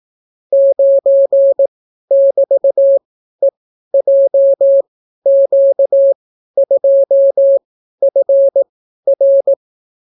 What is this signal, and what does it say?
Morse code, heard as a keyed tone, about 18 wpm, spaced as for 12 wpm, 550 Hz. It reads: 9<BT>EJQ2FR